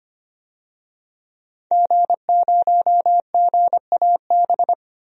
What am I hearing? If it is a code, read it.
Morse code, heard as a keyed tone, about 25 wpm, 700 Hz. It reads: G0GAB
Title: DXCC-100 Call Signs - 5 Characters - 25wpm